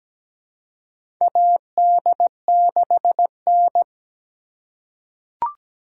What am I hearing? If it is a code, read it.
Morse code, heard as a keyed tone, about 17 wpm, 700 Hz. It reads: AD6N
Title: USA Call Signs - 4 Characters - 2x Character Spacing - 17wpm